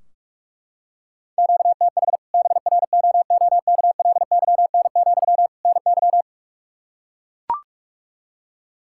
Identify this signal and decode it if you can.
Morse code, heard as a keyed tone, about 41 wpm, 700 Hz. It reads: QTH BROOKLYN, NY
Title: Practice QSO - Exchange 1 - Beginning - 45wpm